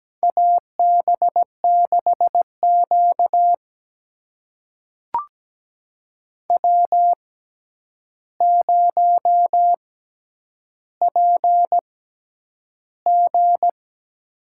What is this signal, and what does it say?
Morse code, heard as a keyed tone, about 17 wpm, spaced as for 9 wpm, 700 Hz. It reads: AB6Q W0PG
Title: USA Call Signs - 4 Characters - 6x Character Spacing - 17wpm